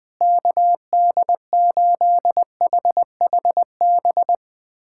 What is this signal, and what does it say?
Morse code, heard as a keyed tone, about 20 wpm, 700 Hz. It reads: KD8HHB
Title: USA Callsign DE Callsign - 6 Characters - 20wpm